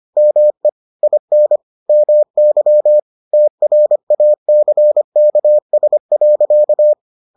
Morse code, heard as a keyed tone, about 25 wpm, 600 Hz. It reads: ME IN MY TRACKS.